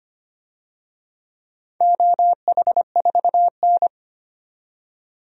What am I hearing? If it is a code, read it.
Morse code, heard as a keyed tone, about 25 wpm, 700 Hz. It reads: OH4N
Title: DXCC-100 Call Signs - 4 Characters - 25wpm